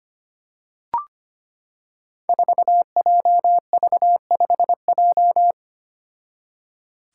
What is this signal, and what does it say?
Morse code, heard as a keyed tone, about 25 wpm, 700 Hz. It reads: 4JV5J
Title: Five Challenging Letter-Number Combinations 25wpm